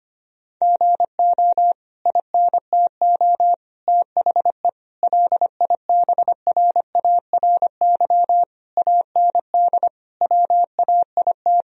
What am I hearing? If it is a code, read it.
Morse code, heard as a keyed tone, about 25 wpm, 700 Hz. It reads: GO INTO THE LIBRARY AND WAIT
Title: Sentences from Top 1000 Words - Review Part 4 - 25wpm